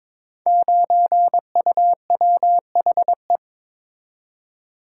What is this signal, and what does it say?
Morse code, heard as a keyed tone, about 22 wpm, 700 Hz. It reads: 9UWHE
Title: Code Groups - 5 Letter:Numbers - 22wpm